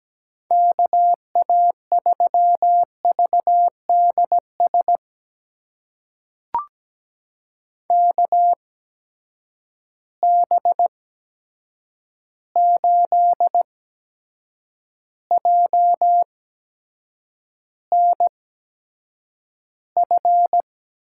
Morse code, heard as a keyed tone, about 17 wpm, spaced as for 7 wpm, 700 Hz. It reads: KA3VDS KB8JNF
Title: USA Call Signs - 6 Characters - 8x Character Spacing - 17wpm